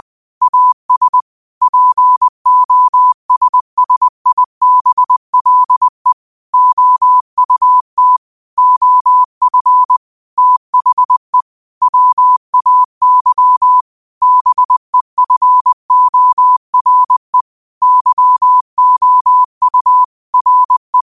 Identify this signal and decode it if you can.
Morse code, heard as a keyed tone, about 20 wpm, 1000 Hz. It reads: AS POSSIBLE OUT OF THE WAY BEFORE YOU RE
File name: 20wpm